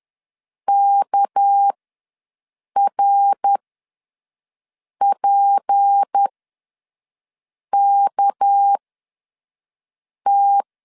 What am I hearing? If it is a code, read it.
Morse code, heard as a keyed tone, about 10 wpm, spaced as for 5 wpm, 800 Hz. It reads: KRPKT
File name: LEZIONI / 6 PKX0 / 6 PK ws 10wpm 3745